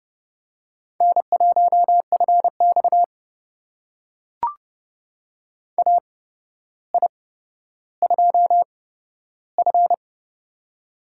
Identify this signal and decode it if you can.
Morse code, heard as a keyed tone, about 30 wpm, spaced as for 13 wpm, 700 Hz. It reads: N1FX AI2F
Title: USA Call Signs - 4 Characters - 8x Character Spacing - 30wpm